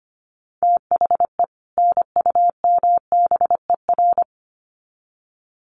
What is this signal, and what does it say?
Morse code, heard as a keyed tone, about 25 wpm, 700 Hz. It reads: THE NUMBER
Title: Sets of 2 Words 25wpm